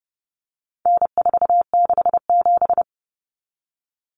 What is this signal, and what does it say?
Morse code, heard as a keyed tone, about 30 wpm, 700 Hz. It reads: N467
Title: Common Cut Numbers - 4-Digit 30wpm